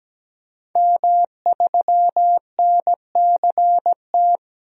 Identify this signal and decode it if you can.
Morse code, heard as a keyed tone, about 17 wpm, 700 Hz. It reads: M3NCT
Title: DXCC-100 Call Signs - 5 Characters - 17wpm